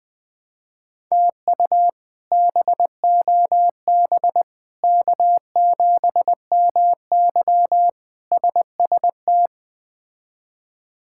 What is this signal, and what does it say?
Morse code, heard as a keyed tone, about 20 wpm, 700 Hz. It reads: TU BOB K7MY SST
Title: K1USN SST QSO Practice - 20wpm